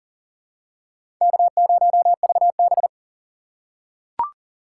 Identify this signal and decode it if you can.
Morse code, heard as a keyed tone, about 40 wpm, 700 Hz. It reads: K0VB
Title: USA Call Signs - 4 Characters - 6x Character Spacing - 40wpm